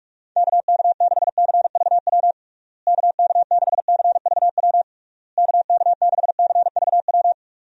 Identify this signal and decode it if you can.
Morse code, heard as a keyed tone, about 46 wpm, 700 Hz. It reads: KK6CVW KK6CVW KK6CVW
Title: USA Call Signs - 6 Characters - Repeated 3x - 45wpm